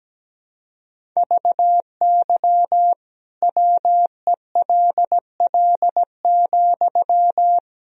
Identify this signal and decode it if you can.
Morse code, heard as a keyed tone, about 17 wpm, 700 Hz. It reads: VY WELL,
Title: Sentences from Top 200 Words and Abbreviations 17wpm